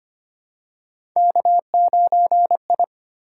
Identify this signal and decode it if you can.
Morse code, heard as a keyed tone, about 25 wpm, 700 Hz. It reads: K9I
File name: Three Letter-Number Combinations - 25wpm